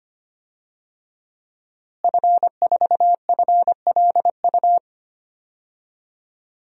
Morse code, heard as a keyed tone, about 25 wpm, 700 Hz. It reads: F4FLU
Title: DXCC-100 Call Signs - 5 Characters - Repeated 3x - 25wpm